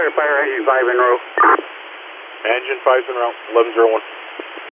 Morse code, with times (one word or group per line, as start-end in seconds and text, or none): none